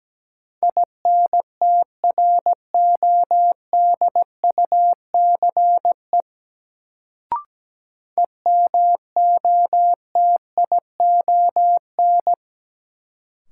Morse code, heard as a keyed tone, about 17 wpm, 700 Hz. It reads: INTRODUCE EMOTION
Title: Sentences from Top 1600 Words - New Word Drill - 17wpm